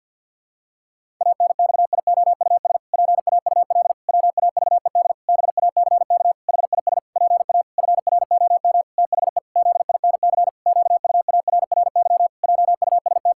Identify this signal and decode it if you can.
Morse code, heard as a keyed tone, about 50 wpm, 700 Hz. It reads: ANXIOUS PAUL WAVED BACK HIS PA FROM THE ZINC QUARRY JUST
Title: Pangram Challenge v2 - 50wpm